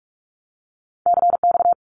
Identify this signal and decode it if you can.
Morse code, heard as a keyed tone, about 46 wpm, 700 Hz. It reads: CX
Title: Abbreviations Repeated 3x - 45wpm